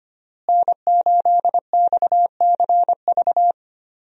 Morse code, heard as a keyed tone, about 25 wpm, 700 Hz. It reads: N8XCV
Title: USA Call Signs - 5 Characters - Repeated 3x - 25wpm